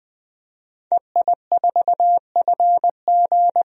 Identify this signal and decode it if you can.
Morse code, heard as a keyed tone, about 20 wpm, 700 Hz. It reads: EI4FG
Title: DXCC-100 Call Signs - 5 Characters - 20wpm